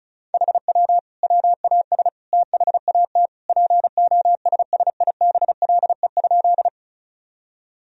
Morse code, heard as a keyed tone, about 36 wpm, 700 Hz. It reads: HW WAS THAT POSSIBLE?
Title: Sentences from Top 500 Words and Abbreviations 35wpm